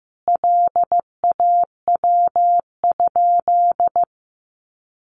Morse code, heard as a keyed tone, about 15 wpm, 700 Hz. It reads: LAW?
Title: Sentences from Top 600 Words 15wpm